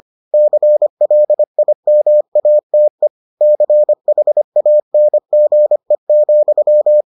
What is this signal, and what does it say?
Morse code, heard as a keyed tone, about 25 wpm, 600 Hz. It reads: CLIMATE CHANGE,